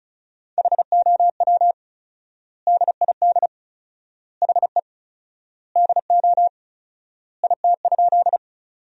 Morse code, heard as a keyed tone, about 34 wpm, 700 Hz. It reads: HOW DID HE DO IT?
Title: 4x Word-Spacing - Sentences from Top 100 Words - 35wpm